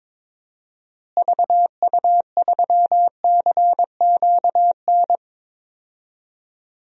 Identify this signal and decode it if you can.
Morse code, heard as a keyed tone, about 22 wpm, 700 Hz. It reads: VU3CQN
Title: DXCC-100 Call Signs - 6 Characters - Repeated 3x - 22wpm